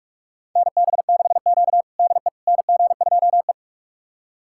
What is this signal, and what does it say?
Morse code, heard as a keyed tone, about 44 wpm, 690 Hz. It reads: NB6Q DE NG2E
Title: USA Callsign DE Callsign - 4 Characters - 45wpm